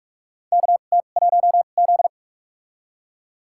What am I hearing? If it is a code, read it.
Morse code, heard as a keyed tone, about 41 wpm, spaced as for 31 wpm, 700 Hz. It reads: KT1Z
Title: USA Call Signs - 4 Characters - 2x Character Spacing - 45wpm